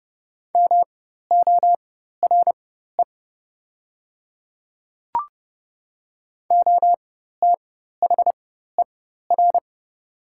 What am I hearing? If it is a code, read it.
Morse code, heard as a keyed tone, about 30 wpm, spaced as for 14 wpm, 700 Hz. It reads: MORE OTHER
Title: 4x Character Spacing - Top 100 Words - 30wpm